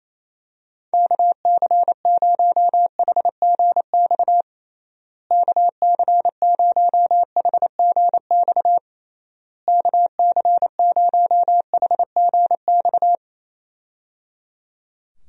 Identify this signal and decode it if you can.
Morse code, heard as a keyed tone, about 28 wpm, 700 Hz. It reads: KC0HGX KC0HGX KC0HGX
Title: USA Call Signs - 6 Characters - Repeated 3x - 28wpm